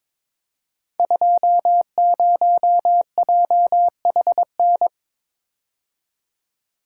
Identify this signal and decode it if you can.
Morse code, heard as a keyed tone, about 23 wpm, 700 Hz. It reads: 20JHN
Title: Code Groups - 5 Letter:Numbers - 22wpm